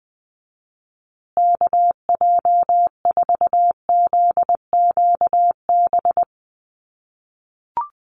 Morse code, heard as a keyed tone, about 20 wpm, 700 Hz. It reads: KJ4ZQB
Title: USA Call Signs Repeated 2x 20wpm